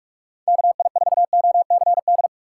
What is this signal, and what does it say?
Morse code, heard as a keyed tone, about 44 wpm, 700 Hz. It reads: KI4OCD
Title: USA Call Signs - 6 Characters - 4x Character Spacing - 45wpm